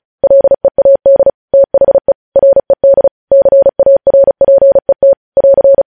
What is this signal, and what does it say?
Morse code, heard as a keyed tone, about 36 wpm, 550 Hz. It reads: LEAD THE RED CARPET <AR>